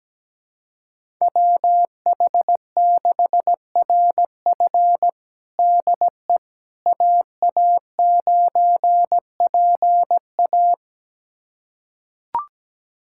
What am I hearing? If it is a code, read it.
Morse code, heard as a keyed tone, about 17 wpm, 700 Hz. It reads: WH6RF DE AA9PA
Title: USA Callsign DE Callsign - 5 Characters - 17wpm